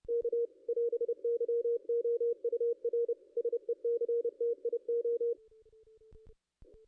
Morse code, heard as a keyed tone, about 29 wpm, 470 Hz. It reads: K <AS>YOUR SECTIO
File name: contests / ss-cw-k5tr-op-n5rz -2100